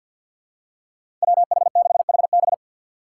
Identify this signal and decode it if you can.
Morse code, heard as a keyed tone, about 52 wpm, 700 Hz. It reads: WH6HB
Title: USA Call Signs - 2x Character Spacing - 50wpm